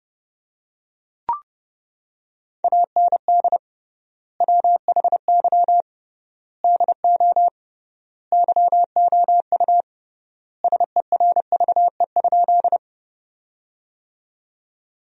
Morse code, heard as a keyed tone, about 32 wpm, 700 Hz. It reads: AND WHY DO YOU SERVE?
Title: 3x Word-Spacing - Sentences from Top 400 Words - 30wpm